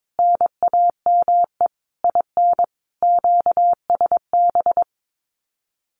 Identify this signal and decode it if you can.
Morse code, heard as a keyed tone, about 22 wpm, 700 Hz. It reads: NAME IN QSB
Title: Common QSO Phrases 22wpm